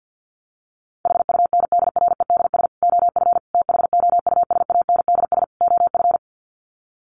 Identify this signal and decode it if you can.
Morse code, heard as a keyed tone, about 50 wpm, 700 Hz. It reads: HUNDREDS OF THOUSANDS OF
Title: Sets of 4 Words 50wpm